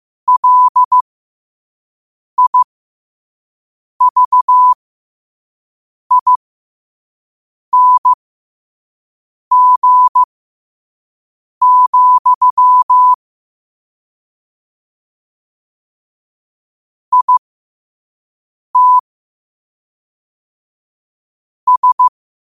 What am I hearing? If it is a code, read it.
Morse code, heard as a keyed tone, about 15 wpm, spaced as for 5 wpm, 1000 Hz. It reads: LIVING, IT S
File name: qotdfiles / 5wpm